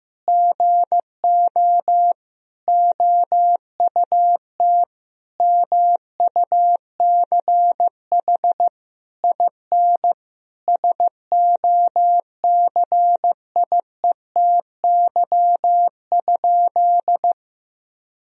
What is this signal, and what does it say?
Morse code, heard as a keyed tone, about 15 wpm, 700 Hz. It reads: GO OUT MUCH IN SOCIETY?
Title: Sentences from Top 1000 Words - Review Part 4 - 15wpm